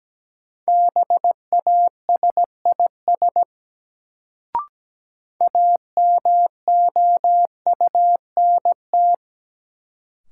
Morse code, heard as a keyed tone, about 17 wpm, 700 Hz. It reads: BASIS AMOUNT
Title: Sets of 3 Words - New Word Drill - 17wpm